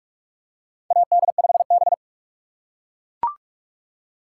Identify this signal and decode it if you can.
Morse code, heard as a keyed tone, about 44 wpm, 700 Hz. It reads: AD5B
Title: USA Call Signs - 4 Characters - 4x Character Spacing - 45wpm